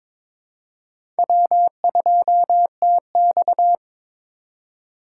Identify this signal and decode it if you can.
Morse code, heard as a keyed tone, about 22 wpm, 700 Hz. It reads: W2TX
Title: USA Call Signs - 4 Characters - 4x Character Spacing - 22wpm